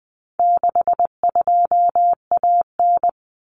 Morse code, heard as a keyed tone, about 20 wpm, 700 Hz. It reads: 62AN
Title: Common Cut Numbers - 4-Digit 20wpm